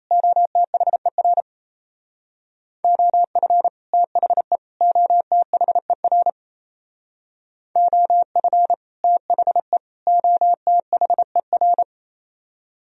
Morse code, heard as a keyed tone, about 34 wpm, 700 Hz. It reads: OTHER OF THE OTHER OF THE OTHER
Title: Speed-Racing - Sets of 3 Words 28wpm